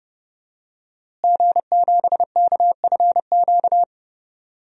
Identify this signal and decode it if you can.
Morse code, heard as a keyed tone, about 30 wpm, 700 Hz. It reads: G7KFQ
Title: DXCC-100 Call Signs - 5 Characters - Repeated 2x - 30wpm